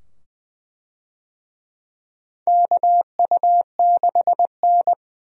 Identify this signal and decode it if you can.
Morse code, heard as a keyed tone, about 20 wpm, 700 Hz. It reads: KU6N
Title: USA Call Signs - 4 Characters - 20wpm